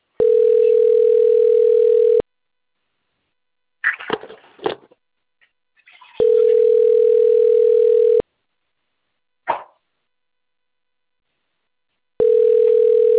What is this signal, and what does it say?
Ringback tone (North America), 440 + 480 Hz.
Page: Sao Carlos